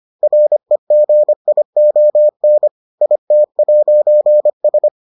Morse code, heard as a keyed tone, about 25 wpm, 600 Hz. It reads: REGION IT'S